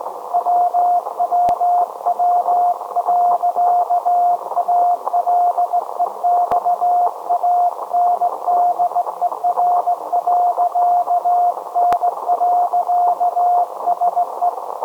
Morse code, heard as a keyed tone, about 18 wpm, 710 Hz. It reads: WWW.ALEXANDER.N.SE
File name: storage / message Grimeton SAQ